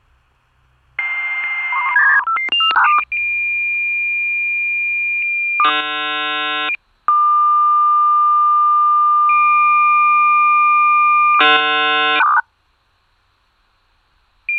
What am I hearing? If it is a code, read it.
Dial-up modem handshake.
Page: Recording provided by Marcus Horndt via Freesound.